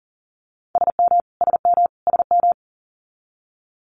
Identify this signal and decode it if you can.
Morse code, heard as a keyed tone, about 40 wpm, 700 Hz. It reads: SM SM SM
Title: Multi-letter Phonemes Repeated 3x 40wpm